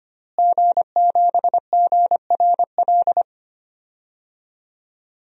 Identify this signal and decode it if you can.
Morse code, heard as a keyed tone, about 25 wpm, 700 Hz. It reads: G7GRL
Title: DXCC-100 Call Signs - 5 Characters - Repeated 2x - 25wpm